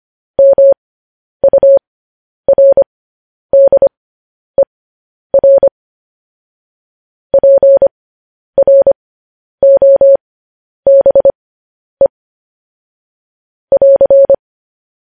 Morse code, heard as a keyed tone, about 25 wpm, spaced as for 10 wpm, 550 Hz. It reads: MURDER PROBE <AR>
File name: podcast / News headlines-1x-25x10